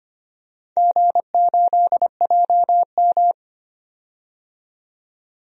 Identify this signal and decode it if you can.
Morse code, heard as a keyed tone, about 25 wpm, 690 Hz. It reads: G8JM
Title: DXCC-100 Call Signs - 4 Characters - 25wpm